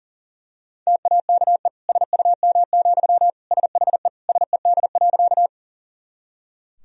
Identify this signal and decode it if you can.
Morse code, heard as a keyed tone, about 37 wpm, 700 Hz. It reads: TAKE SUM, SHE SED.
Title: Sentences from Top 200 Words and Abbreviations 40wpm